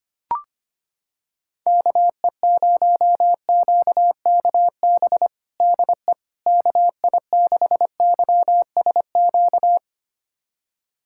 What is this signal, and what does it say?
Morse code, heard as a keyed tone, about 25 wpm, 700 Hz. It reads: KE0QKB DE KI6YSQ
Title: USA Callsign DE Callsign - 6 Characters - 25wpm